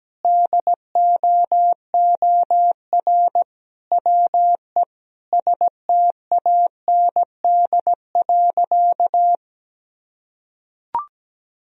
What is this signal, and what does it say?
Morse code, heard as a keyed tone, about 17 wpm, 700 Hz. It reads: DOOR WE STAND.
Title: Sentences from Top 1000 Words - Review Part 1 - 17wpm